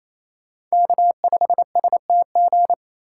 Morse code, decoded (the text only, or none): K5STG